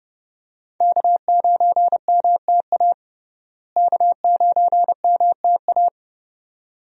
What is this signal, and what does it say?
Morse code, heard as a keyed tone, about 32 wpm, 700 Hz. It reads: K9MTA K9MTA
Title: USA Call Signs - 5 Characters - Repeated 2x - 30wpm